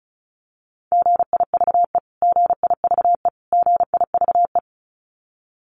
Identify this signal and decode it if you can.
Morse code, heard as a keyed tone, about 34 wpm, 700 Hz. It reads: GIVE GIVE GIVE
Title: Top 200 Words Repeated 35wpm